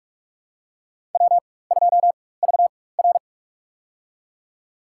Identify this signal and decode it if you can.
Morse code, heard as a keyed tone, about 47 wpm, spaced as for 21 wpm, 700 Hz. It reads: W2VR
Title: USA Call Signs - 4 Characters - 4x Character Spacing - 45wpm